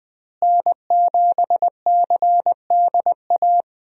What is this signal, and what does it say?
Morse code, heard as a keyed tone, about 20 wpm, 700 Hz. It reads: N7CDA
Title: USA Call Signs - 5 Characters - Repeated 2x - 20wpm